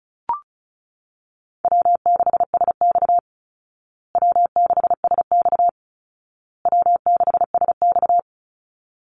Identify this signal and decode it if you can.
Morse code, heard as a keyed tone, about 34 wpm, 700 Hz. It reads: W6SX W6SX W6SX
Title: Easy Call Signs Repeated 3x - 35wpm